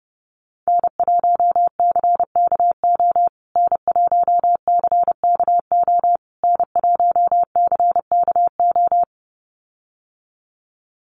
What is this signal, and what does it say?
Morse code, heard as a keyed tone, about 30 wpm, 700 Hz. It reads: N1CKO N1CKO N1CKO
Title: USA Call Signs Repeated 3x 30wpm